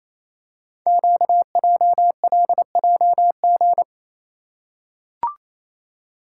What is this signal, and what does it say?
Morse code, heard as a keyed tone, about 29 wpm, 700 Hz. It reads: QJLJG
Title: Code Groups - 5 Letters - 28wpm